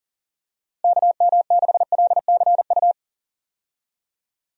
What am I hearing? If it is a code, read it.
Morse code, heard as a keyed tone, about 37 wpm, 700 Hz. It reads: KM6LCU
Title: USA Call Signs - 6 Characters - Repeated 3x - 40wpm